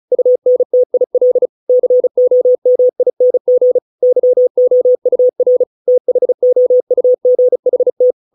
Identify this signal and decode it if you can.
Morse code, heard as a keyed tone, about 34 wpm, 500 Hz. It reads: UNTIL COMING YOUR THOUGHT